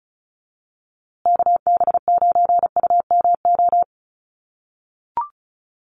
Morse code, heard as a keyed tone, about 34 wpm, 700 Hz. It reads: KB9UMO
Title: USA Call Signs Repeated 2x 35wpm